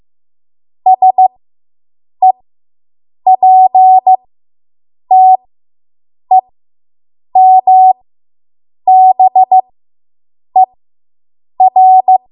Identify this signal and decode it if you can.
Morse code, heard as a keyed tone, about 15 wpm, spaced as for 7 wpm, 750 Hz. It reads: SEPTEMBER